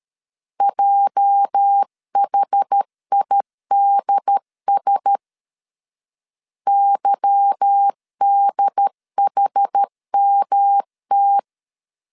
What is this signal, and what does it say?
Morse code, heard as a keyed tone, about 13 wpm, 800 Hz. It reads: JHIDS YDHMT
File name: LEZIONI / 9 286 12wpm 2305